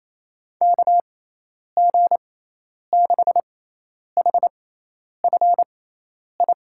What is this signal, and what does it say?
Morse code, heard as a keyed tone, about 28 wpm, spaced as for 10 wpm, 700 Hz. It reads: KG6HFI
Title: USA Call Signs - 6 Characters - 6x Character Spacing - 28wpm